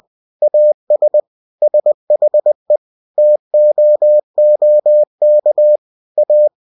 Morse code, heard as a keyed tone, about 20 wpm, 600 Hz. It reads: AS SHE TOOK A
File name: cw-books / winnie-the-pooh / winnie-20wpm-page-0072